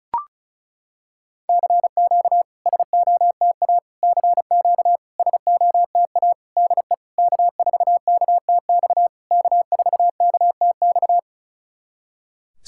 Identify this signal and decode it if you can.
Morse code, heard as a keyed tone, about 34 wpm, 690 Hz. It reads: CQ SOTA CQ SOTA DE K4KTX K4KTX
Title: SOTA QSO Practice 35wpm